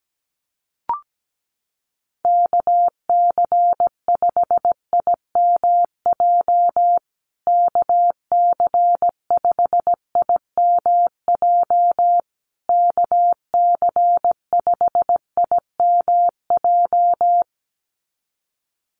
Morse code, heard as a keyed tone, about 17 wpm, 700 Hz. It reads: KC5IMJ KC5IMJ KC5IMJ
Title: USA Call Signs Repeated 3x 17wpm